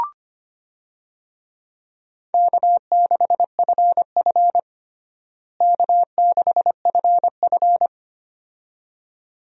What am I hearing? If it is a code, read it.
Morse code, heard as a keyed tone, about 25 wpm, 700 Hz. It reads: K6FF K6FF
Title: USA Call Signs - 4 Characters - Repeated 2x - 25wpm